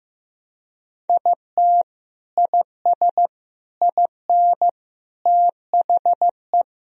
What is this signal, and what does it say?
Morse code, heard as a keyed tone, about 16 wpm, 700 Hz. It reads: IT IS IN THE
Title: Sentences from Top 1000 Words - Review Part 4 - 15wpm